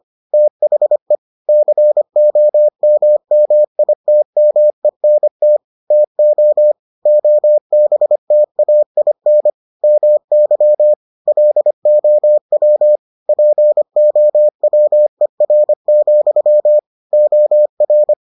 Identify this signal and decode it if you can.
Morse code, heard as a keyed tone, about 25 wpm, 600 Hz. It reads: THE COMMITMENT TO OBTAIN MY LOW POWER, OR